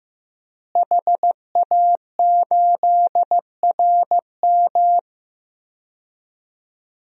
Morse code, heard as a keyed tone, about 15 wpm, 700 Hz. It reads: HA8RM
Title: DXCC-100 Call Signs - 5 Characters - Repeated 2x - 15wpm